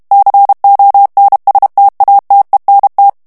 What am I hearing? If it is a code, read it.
Morse code, heard as a keyed tone, about 32 wpm, 800 Hz. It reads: CONSTATENT
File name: morse mp3 / FRANCAIS TEXTE CLAIR 1 VITESSE 30 MOTS MINUTES